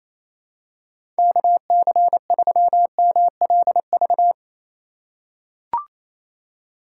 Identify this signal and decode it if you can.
Morse code, heard as a keyed tone, about 28 wpm, 700 Hz. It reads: KC3MLV
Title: USA Call Signs - 6 Characters - 8x Character Spacing - 28wpm